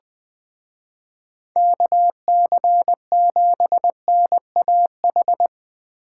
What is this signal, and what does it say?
Morse code, heard as a keyed tone, about 20 wpm, 700 Hz. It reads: KC7NAH
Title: Speed Racing - USA 2x3 Call Signs 17wpm